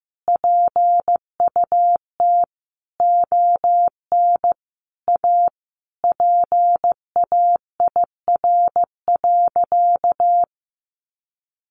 Morse code, heard as a keyed tone, about 14 wpm, 700 Hz. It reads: PUT ON A PAIR.
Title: Sentences from Top 1200 Words 15wpm